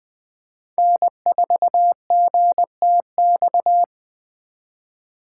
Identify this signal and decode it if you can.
Morse code, heard as a keyed tone, about 23 wpm, 700 Hz. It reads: N4GTX
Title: Speed Racing - USA 1x3 Call Signs 20wpm